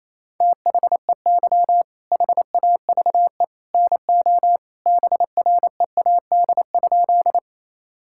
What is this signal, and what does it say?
Morse code, heard as a keyed tone, about 28 wpm, 700 Hz. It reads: THEY HAVE NO BREAD?
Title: Sentences from Top 1600 Words 28wpm